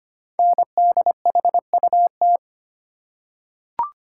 Morse code, heard as a keyed tone, about 25 wpm, 700 Hz. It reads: NDHUT
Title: Code Groups - 5 Letters - 25wpm